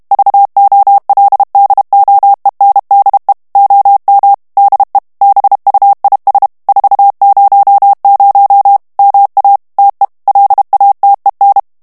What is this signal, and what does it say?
Morse code, heard as a keyed tone, about 32 wpm, 800 Hz. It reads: VOLDOENDE OM DE BUIS 400 MA TE LATEN
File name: HOLLANDAIS texte clair 2 vitesse 30 mots minute